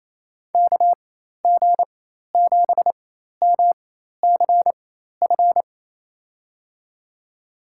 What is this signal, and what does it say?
Morse code, heard as a keyed tone, about 28 wpm, spaced as for 13 wpm, 700 Hz. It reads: KG7MCF
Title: USA Call Signs - 6 Characters - 4x Character Spacing - 28wpm